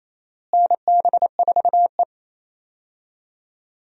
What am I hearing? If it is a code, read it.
Morse code, heard as a keyed tone, about 28 wpm, 700 Hz. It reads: NB4E
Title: USA Call Signs - 4 Characters - Repeated 2x - 28wpm